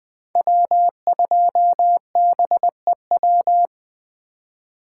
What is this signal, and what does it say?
Morse code, heard as a keyed tone, about 20 wpm, 700 Hz. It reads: W2BEW
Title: USA Call Signs - 5 Characters - 2x Character Spacing - 20wpm